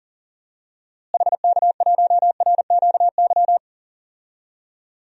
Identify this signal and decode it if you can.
Morse code, heard as a keyed tone, about 40 wpm, 700 Hz. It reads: HK1RQY
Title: DXCC-100 Call Signs - 6 Characters - Repeated 2x - 40wpm